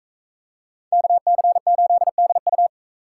Morse code, heard as a keyed tone, about 40 wpm, 700 Hz. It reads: KC8DU